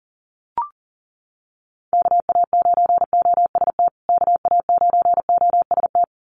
Morse code, heard as a keyed tone, about 40 wpm, 700 Hz. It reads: KA9OST KA9OST
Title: USA Call Signs Repeated 2x 40wpm